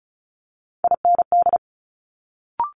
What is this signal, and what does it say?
Morse code, heard as a keyed tone, about 36 wpm, 700 Hz. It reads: IND
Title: Multi-letter Phonemes Quick Fire 35wpm